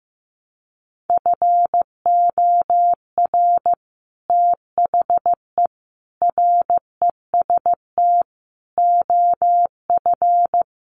Morse code, heard as a keyed tone, about 14 wpm, 700 Hz. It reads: FOR THE REST OF
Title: Sets of 5 Words Encore2 15wpm